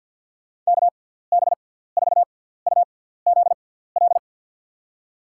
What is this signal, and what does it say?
Morse code, heard as a keyed tone, about 50 wpm, spaced as for 17 wpm, 700 Hz. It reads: KB4UZL
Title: USA Call Signs - 6 Characters - 6x Character Spacing - 50wpm